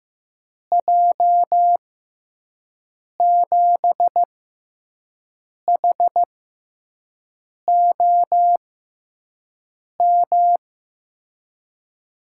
Morse code, heard as a keyed tone, about 15 wpm, spaced as for 5 wpm, 700 Hz. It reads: J7HOM
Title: USA Call Signs - 6 Characters - 6x Character Spacing - 15wpm